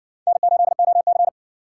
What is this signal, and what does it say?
Morse code, heard as a keyed tone, about 60 wpm, 700 Hz. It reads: N8OC